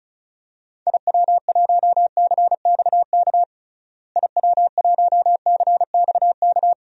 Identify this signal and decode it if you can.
Morse code, heard as a keyed tone, about 34 wpm, 700 Hz. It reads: IW1CXK IW1CXK
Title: DXCC-100 Call Signs - 6 Characters - Repeated 2x - 35wpm